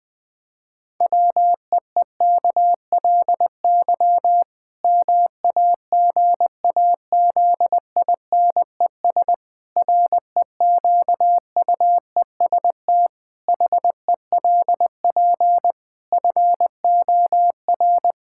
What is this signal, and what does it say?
Morse code, heard as a keyed tone, about 20 wpm, 700 Hz. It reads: WEEKLY MAGAZINES REQUEST HELP FOR
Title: Pangram Challenge v2 - 20wpm